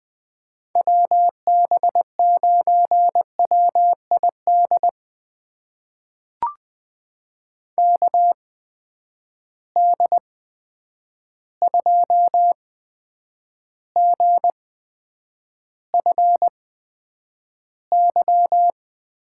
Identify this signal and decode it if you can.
Morse code, heard as a keyed tone, about 21 wpm, spaced as for 9 wpm, 700 Hz. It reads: WB9WID KD2GFY
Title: USA Call Signs - 6 Characters - 8x Character Spacing - 20wpm